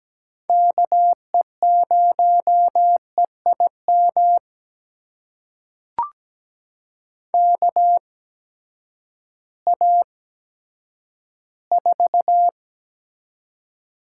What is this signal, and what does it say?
Morse code, heard as a keyed tone, about 17 wpm, spaced as for 10 wpm, 700 Hz. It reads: KE0EIM KA4
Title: USA Call Signs - 6 Characters - 8x Character Spacing - 17wpm